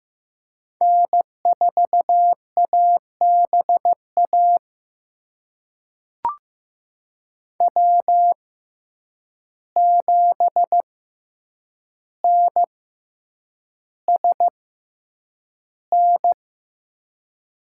Morse code, heard as a keyed tone, about 15 wpm, spaced as for 8 wpm, 700 Hz. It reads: N4ABA W7NSN
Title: USA Call Signs - 5 Characters - 6x Character Spacing - 15wpm